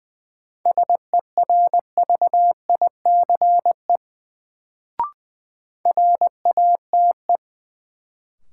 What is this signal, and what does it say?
Morse code, heard as a keyed tone, about 21 wpm, 700 Hz. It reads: SERVICE RATE
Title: Sentences from Top 700 Words - New Word Drill - 20wpm